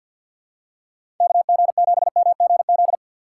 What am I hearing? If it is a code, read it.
Morse code, heard as a keyed tone, about 53 wpm, 700 Hz. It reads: KG7MGZ
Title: USA Call Signs - 6 Characters - 4x Character Spacing - 50wpm